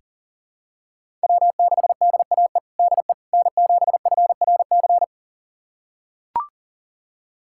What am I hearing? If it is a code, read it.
Morse code, heard as a keyed tone, about 40 wpm, 700 Hz. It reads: W6DAE DE N7FRC
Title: USA Callsign DE Callsign - 5 Characters - 40wpm